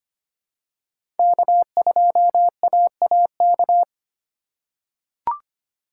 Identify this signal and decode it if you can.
Morse code, heard as a keyed tone, about 25 wpm, 700 Hz. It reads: K2AAK
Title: USA Call Signs - 5 Characters - 4x Character Spacing - 25wpm